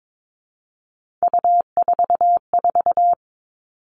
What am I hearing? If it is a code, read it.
Morse code, heard as a keyed tone, about 22 wpm, 700 Hz. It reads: U44
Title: All Cut Numbers - 3-Digit 22wpm